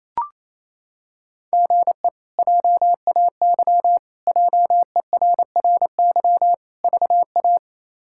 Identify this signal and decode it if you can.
Morse code, heard as a keyed tone, about 28 wpm, 700 Hz. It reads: GE JAY JERRY VA
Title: K1USN SST QSO Practice - 28wpm